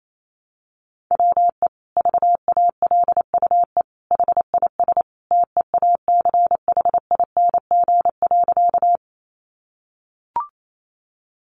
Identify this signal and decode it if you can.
Morse code, heard as a keyed tone, about 28 wpm, 700 Hz. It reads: WE VALUE HIS TEACHING.
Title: Sentences from Top 1100 Words 28wpm